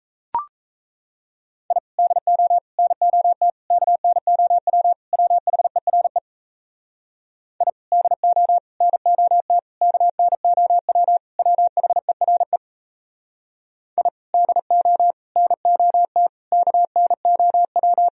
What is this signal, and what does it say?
Morse code, heard as a keyed tone, about 37 wpm, 700 Hz. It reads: I DO NOT KNOW WHERE I DO NOT KNOW WHERE I DO NOT KNOW